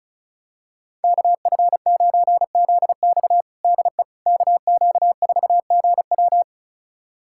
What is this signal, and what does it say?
Morse code, heard as a keyed tone, about 34 wpm, 700 Hz. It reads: KF9ZX DE KQ4GW
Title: USA Callsign DE Callsign - 5 Characters - 35wpm